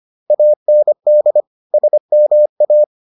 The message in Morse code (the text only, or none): AND SMA